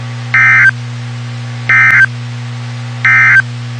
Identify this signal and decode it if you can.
Emergency Alert System (SAME) tones, heard as harsh buzzing bursts. It is heard from 0.2 to 3.5 s.
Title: WXK91 NOAA Weather Radio